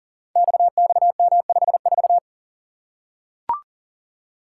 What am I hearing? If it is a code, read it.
Morse code, heard as a keyed tone, about 40 wpm, 700 Hz. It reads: XXM54
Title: Code Groups - 5 Letter:Numbers - 40wpm